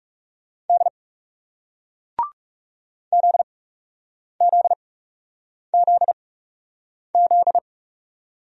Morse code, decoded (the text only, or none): D ZZZZ